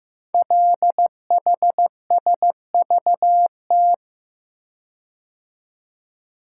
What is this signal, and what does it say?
Morse code, heard as a keyed tone, about 15 wpm, 700 Hz. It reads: LHSVT
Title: Code Groups - 5 Letters - 15wpm